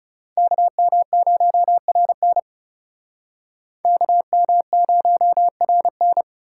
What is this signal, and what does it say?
Morse code, heard as a keyed tone, about 40 wpm, 700 Hz. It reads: KM0RN KM0RN